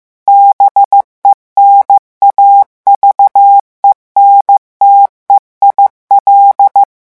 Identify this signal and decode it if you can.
Morse code, heard as a keyed tone, about 15 wpm, 800 Hz. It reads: BENAVENTE IL
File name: morse mp3 / revision9 citations humoristiques vitesse 14 mots minute